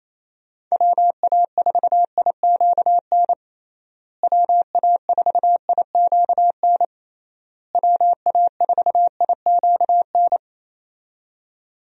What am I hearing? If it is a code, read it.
Morse code, heard as a keyed tone, about 28 wpm, 700 Hz. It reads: WA4IQN WA4IQN WA4IQN
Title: USA Call Signs - 6 Characters - Repeated 3x - 28wpm